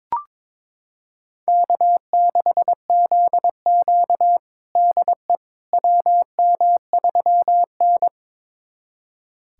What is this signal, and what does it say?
Morse code, heard as a keyed tone, about 22 wpm, 700 Hz. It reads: K6ZQ DE WM3N
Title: USA Callsign DE Callsign - 4 Characters - 22wpm